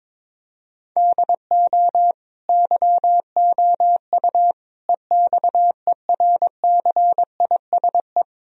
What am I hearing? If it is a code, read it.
Morse code, heard as a keyed tone, about 22 wpm, 700 Hz. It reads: DO YOU EXERCISE
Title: Sentences from Top 1000 Words - Review Part 4 - 22wpm